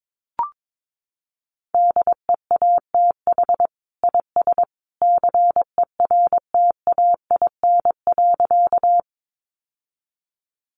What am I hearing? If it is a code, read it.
Morse code, heard as a keyed tone, about 22 wpm, 700 Hz. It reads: DEATH IS CERTAIN.
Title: Sentences from Top 1100 Words 22wpm